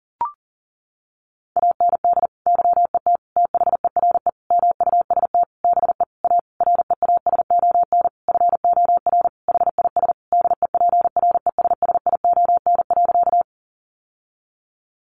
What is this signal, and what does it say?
Morse code, heard as a keyed tone, about 40 wpm, 700 Hz. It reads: AND YET THERE MUST BE A REASON FOR HIS DEPRESSION.
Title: Sentences from Top 1300 Words 40wpm